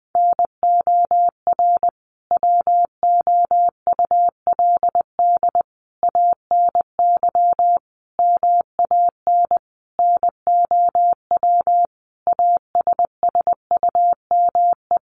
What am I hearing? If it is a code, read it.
Morse code, heard as a keyed tone, about 20 wpm, 700 Hz. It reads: NOR WOULD ANY MAN NOW ASSUME
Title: Sentences from Top 1500 Words 20wpm